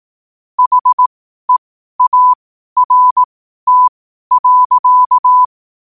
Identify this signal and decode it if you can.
Morse code, heard as a keyed tone, about 18 wpm, spaced as for 12 wpm, 1000 Hz. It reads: HEART.
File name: qotdfiles / 12wpm